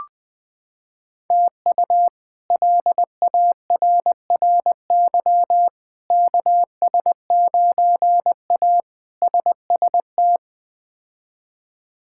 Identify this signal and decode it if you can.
Morse code, heard as a keyed tone, about 21 wpm, 700 Hz. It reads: TU LARRY KS9A SST
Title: K1USN SST QSO Practice - 20wpm